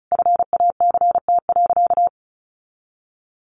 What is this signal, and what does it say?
Morse code, heard as a keyed tone, about 34 wpm, 700 Hz. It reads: FACT.
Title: Sentences from Top 1200 Words 35wpm